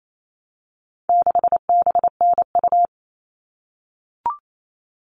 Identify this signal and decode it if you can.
Morse code, heard as a keyed tone, about 28 wpm, 700 Hz. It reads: 6BNU
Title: All Cut Numbers - 4-Digit 28wpm